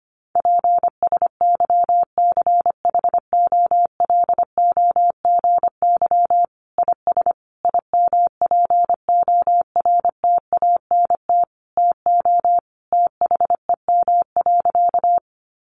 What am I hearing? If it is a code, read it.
Morse code, heard as a keyed tone, about 25 wpm, 700 Hz. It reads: PSYCHOLOGY IS IMPORTANT TO THEM.
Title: Sentences from Top 1400 Words 25wpm